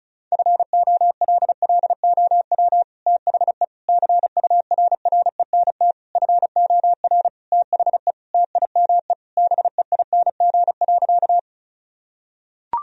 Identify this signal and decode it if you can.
Morse code, heard as a keyed tone, about 34 wpm, 700 Hz. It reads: FOLLOW THE CURRENT FOR THE TIME BEING.
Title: Sentences from Top 1000 Words - Review Part 4 - 35wpm